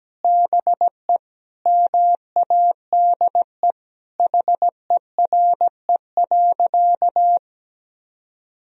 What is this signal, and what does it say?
Morse code, heard as a keyed tone, about 17 wpm, 700 Hz. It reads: BE MADE HERE.
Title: Sentences from Top 1000 Words - Review Part 4 - 17wpm